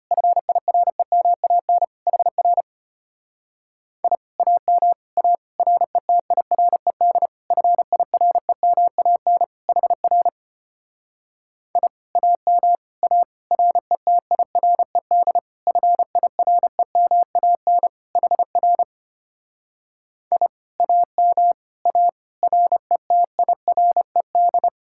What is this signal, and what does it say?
Morse code, heard as a keyed tone, about 29 wpm, 700 Hz. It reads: FIREMAN HR I AM A RETIRED FIREMAN HR I AM A RETIRED FIREMAN HR I AM A RETIRED
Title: Speed-Racing Common QSO Phrases - 25wpm